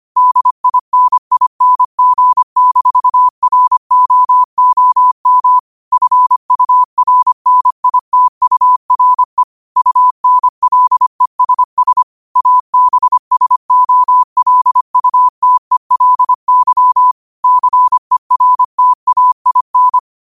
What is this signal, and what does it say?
Morse code, heard as a keyed tone, about 25 wpm, 1000 Hz. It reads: DINING-ROOM FURNITURE UNLESS ABSOLUTELY CERTAIN
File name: qotdfiles / 25wpm